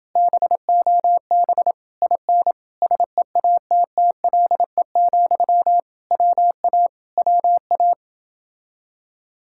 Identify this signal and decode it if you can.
Morse code, heard as a keyed tone, about 28 wpm, 700 Hz. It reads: BOB IN SEATTLE, WA WA